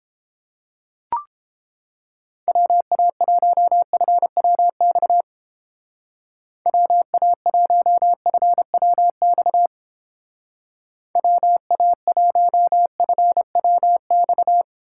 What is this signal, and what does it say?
Morse code, heard as a keyed tone, about 30 wpm, 700 Hz. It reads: WA1FWX WA1FWX WA1FWX